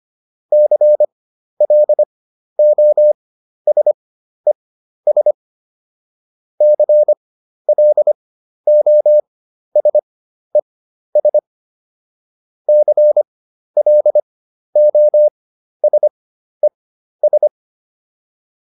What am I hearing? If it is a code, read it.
Morse code, heard as a keyed tone, about 25 wpm, spaced as for 12 wpm, 600 Hz. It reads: CLOSES CLOSES CLOSES